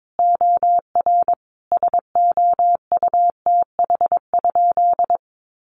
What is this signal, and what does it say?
Morse code, heard as a keyed tone, about 22 wpm, 700 Hz. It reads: OR SOUTH?
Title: Sentences from Top 400 Words 22wpm